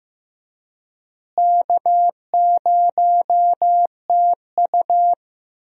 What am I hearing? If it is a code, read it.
Morse code, heard as a keyed tone, about 15 wpm, 700 Hz. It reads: K0TU
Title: USA Call Signs - 4 Characters - Repeated 2x - 15wpm